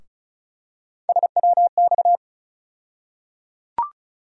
Morse code, heard as a keyed tone, about 34 wpm, 700 Hz. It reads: SWX
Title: USA Call Sign Postfixes - 3 Letters 35wpm